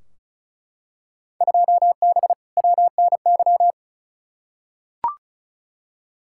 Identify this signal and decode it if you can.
Morse code, heard as a keyed tone, about 36 wpm, 700 Hz. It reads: 2B WNY
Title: Field Day Exchanges Encore - 35wpm